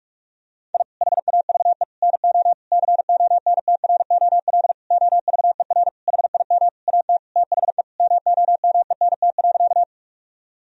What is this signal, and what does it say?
Morse code, heard as a keyed tone, about 41 wpm, 700 Hz. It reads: I HAVE NO CONTROL OVER HIM AT THE MOMENT.
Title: Sentences from Top 1000 Words - Review Part 3 - 45wpm